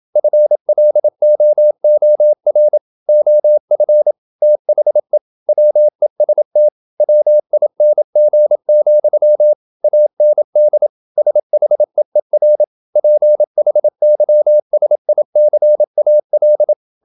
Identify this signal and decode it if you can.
Morse code, heard as a keyed tone, about 27 wpm, 600 Hz. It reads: FLOOR OF THE WEST WING, AND SHEER PHYSICAL
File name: Never 0027wpm